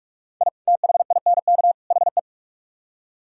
Morse code, heard as a keyed tone, about 43 wpm, 700 Hz. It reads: I THINK HE